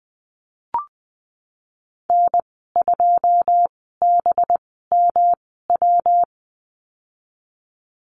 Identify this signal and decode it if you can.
Morse code, heard as a keyed tone, about 20 wpm, spaced as for 15 wpm, 700 Hz. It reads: N2BMW
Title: 207 Lesson 33g - Letter j - Call Signs 20wpm